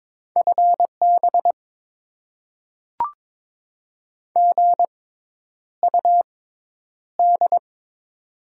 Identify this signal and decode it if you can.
Morse code, heard as a keyed tone, about 22 wpm, spaced as for 10 wpm, 700 Hz. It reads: FB GUD
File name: Top 20 QSO Elements - 6x Character Spacing - 22wpm